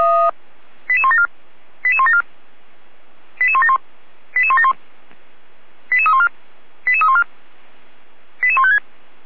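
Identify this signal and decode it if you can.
Five-tone selective call (ZVEI), heard as quick runs of five tones, tones of about 70 ms. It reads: 88174 88174 88171 88171 88314 88314 88156
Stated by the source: ZVEI five-tone call received by SDR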